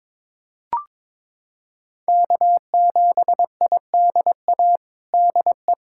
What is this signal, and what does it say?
Morse code, heard as a keyed tone, about 22 wpm, 700 Hz. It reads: K7IDA DE
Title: USA Callsign DE Callsign - 5 Characters - 22wpm